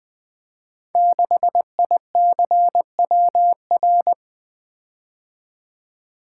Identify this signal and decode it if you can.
Morse code, heard as a keyed tone, about 20 wpm, 700 Hz. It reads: 6ICWR
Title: Code Groups - 5 Letter:Numbers - 20wpm